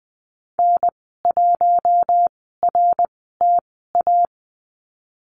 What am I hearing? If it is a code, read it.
Morse code, heard as a keyed tone, about 20 wpm, spaced as for 15 wpm, 700 Hz. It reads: N1RTA
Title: 055 Lesson 10g - Letter r - Call Signs 20wpm